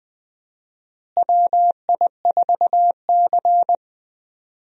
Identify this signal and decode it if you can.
Morse code, heard as a keyed tone, about 20 wpm, 700 Hz. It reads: WI4C
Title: USA Call Signs - 4 Characters - Repeated 3x - 20wpm